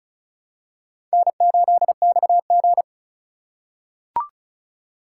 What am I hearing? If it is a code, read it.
Morse code, heard as a keyed tone, about 36 wpm, 700 Hz. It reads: N8XG
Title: USA Call Signs - 4x Character Spacing - 35wpm